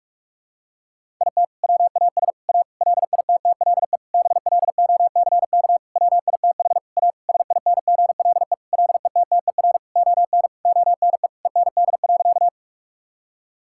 Morse code, heard as a keyed tone, about 43 wpm, 700 Hz. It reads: IT WAS A LITTLE BLOCK WITH A SINGLE LETTER ON ONE END.
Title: Sentences from Top 1600 Words 45wpm